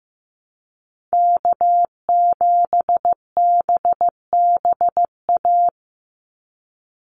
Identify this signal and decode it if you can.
Morse code, heard as a keyed tone, about 15 wpm, 700 Hz. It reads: K7BBA
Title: USA Call Signs 15wpm